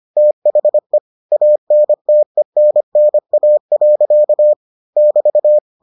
Morse code, heard as a keyed tone, about 25 wpm, 600 Hz. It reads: THE ANTENNA. <BT>